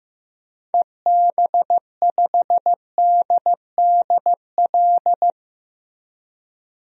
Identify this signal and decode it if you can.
Morse code, heard as a keyed tone, about 15 wpm, 700 Hz. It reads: EB5DDL
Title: DXCC-100 Call Signs - 6 Characters - Repeated 2x - 15wpm